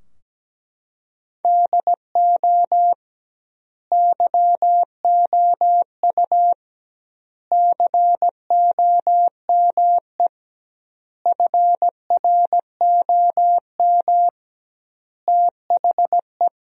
Morse code, heard as a keyed tone, about 17 wpm, 700 Hz. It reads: DO YOU COME FROM THE
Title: 2x Word-Spacing - Sentences from Top 200 Words - 17wpm